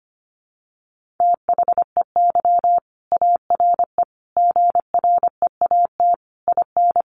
Morse code, heard as a keyed tone, about 25 wpm, 700 Hz. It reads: THEY ARE GREAT IN
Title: Sentences from Top 1400 Words 25wpm